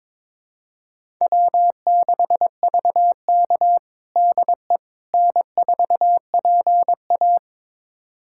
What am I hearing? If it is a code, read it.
Morse code, heard as a keyed tone, about 22 wpm, 700 Hz. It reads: W6VK DE N4PA
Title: USA Callsign DE Callsign - 4 Characters - 22wpm